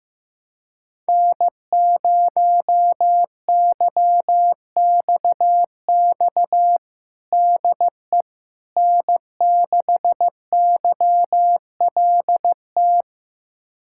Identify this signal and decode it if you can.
Morse code, heard as a keyed tone, about 16 wpm, 700 Hz. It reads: N0YXX DE N6YLT
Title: USA Callsign DE Callsign - 5 Characters - 15wpm